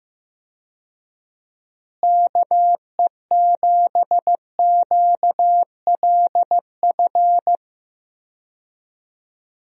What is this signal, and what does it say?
Morse code, heard as a keyed tone, about 15 wpm, 700 Hz. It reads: KE7QLF
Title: USA Call Signs - 6 Characters - Repeated 2x - 15wpm